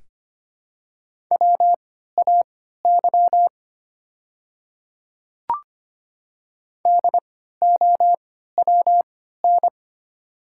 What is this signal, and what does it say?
Morse code, heard as a keyed tone, about 25 wpm, spaced as for 14 wpm, 700 Hz. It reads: WAY DOWN
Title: 3x Character Spacing - Top 100 Words - 25wpm